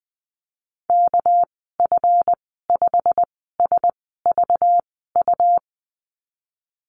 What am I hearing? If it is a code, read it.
Morse code, heard as a keyed tone, about 20 wpm, spaced as for 15 wpm, 700 Hz. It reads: KF5SVU
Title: 200 Lesson 32g - Letter k - Call Signs 20wpm